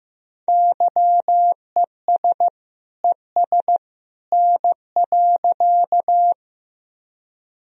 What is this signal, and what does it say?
Morse code, heard as a keyed tone, about 15 wpm, 700 Hz. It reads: YES ES N.
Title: Sentences from Top 500 Words and Abbreviations 15wpm